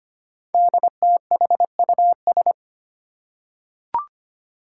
Morse code, heard as a keyed tone, about 25 wpm, 700 Hz. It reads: DTHUS
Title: Five Challenging Letter-Number Combinations 25wpm